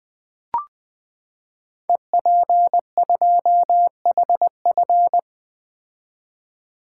Morse code, heard as a keyed tone, about 20 wpm, 700 Hz. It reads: EP2HF
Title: Code Groups - 5 Letter:Numbers - 20wpm